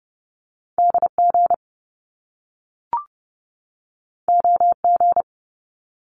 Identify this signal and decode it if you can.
Morse code, heard as a keyed tone, about 30 wpm, 700 Hz. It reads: DG OG
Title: Top Two Letter Combinations in Words 30wpm